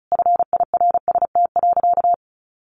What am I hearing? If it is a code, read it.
Morse code, heard as a keyed tone, about 34 wpm, 700 Hz. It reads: FIRST.
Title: Sentences from Top 900 Words 35wpm